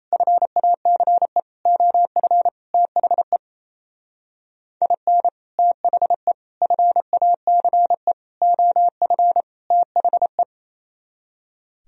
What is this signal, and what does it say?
Morse code, heard as a keyed tone, about 35 wpm, 700 Hz. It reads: FACE OF THE IN THE FACE OF THE
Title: Speed-Racing - Sets of 5 Words 28wpm